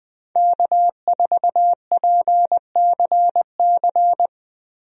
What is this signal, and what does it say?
Morse code, heard as a keyed tone, about 21 wpm, 700 Hz. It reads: K4PCC
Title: USA Call Signs - 5 Characters - Repeated 3x - 20wpm